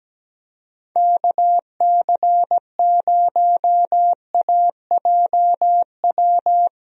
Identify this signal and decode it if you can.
Morse code, heard as a keyed tone, about 17 wpm, 700 Hz. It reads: KC0AJW
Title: USA Call Signs - 6 Characters - Repeated 3x - 17wpm